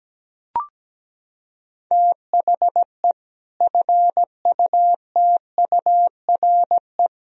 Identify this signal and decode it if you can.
Morse code, heard as a keyed tone, about 17 wpm, 700 Hz. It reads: THE FUTURE
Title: Sentences from Top 1000 Words - Review Part 3 - 17wpm